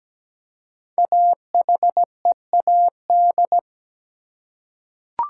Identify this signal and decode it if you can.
Morse code, heard as a keyed tone, about 17 wpm, 700 Hz. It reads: AHEAD
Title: Five Letter Words - 17wpm